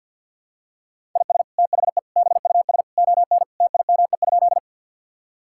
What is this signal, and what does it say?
Morse code, heard as a keyed tone, about 52 wpm, 700 Hz. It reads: IS THE BUS ON TIME?
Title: Sentences from Top 1000 Words - Review Part 6 - 50wpm